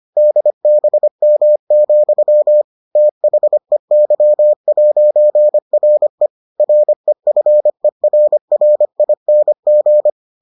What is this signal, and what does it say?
Morse code, heard as a keyed tone, about 25 wpm, 600 Hz. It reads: DBM, THEY'RE REFERRING